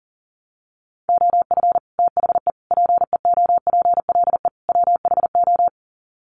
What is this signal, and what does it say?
Morse code, heard as a keyed tone, about 37 wpm, 700 Hz. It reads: OF THE PEOPLE WHO
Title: Sets of 4 Words 40wpm